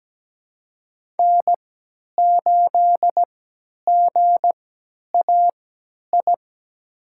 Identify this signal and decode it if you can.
Morse code, heard as a keyed tone, about 17 wpm, spaced as for 10 wpm, 700 Hz. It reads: N8GAI
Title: USA Call Signs - 5 Characters - 3x Character Spacing - 17wpm